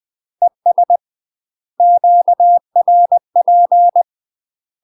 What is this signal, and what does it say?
Morse code, heard as a keyed tone, about 20 wpm, 700 Hz. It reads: ES QRP
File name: UK MASTER QSOs TRAINING / 20WPM / UK MASTER QSOs 0042